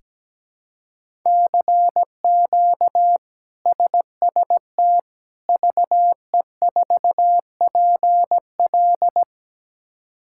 Morse code, heard as a keyed tone, about 17 wpm, 700 Hz. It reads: CQ SST VE4PL
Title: K1USN SST QSO Practice - 17wpm